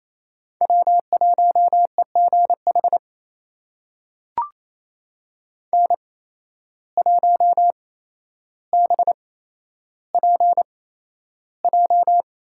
Morse code, heard as a keyed tone, about 28 wpm, spaced as for 12 wpm, 700 Hz. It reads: W1EGH N1BPJ
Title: USA Call Signs - 5 Characters - 8x Character Spacing - 28wpm